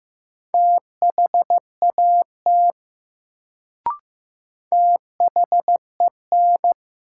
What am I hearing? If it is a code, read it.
Morse code, heard as a keyed tone, about 15 wpm, 700 Hz. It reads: THAT THEN
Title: Sentences from Top 100 Words - New Word Drill - 15wpm